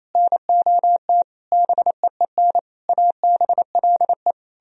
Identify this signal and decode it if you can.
Morse code, heard as a keyed tone, about 36 wpm, 700 Hz. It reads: NOT BEEN ABLE